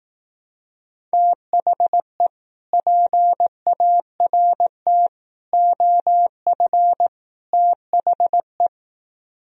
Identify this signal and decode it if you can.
Morse code, heard as a keyed tone, about 18 wpm, 700 Hz. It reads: THE PART OF THE
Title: Speed-Racing - Sets of 4 Words 15wpm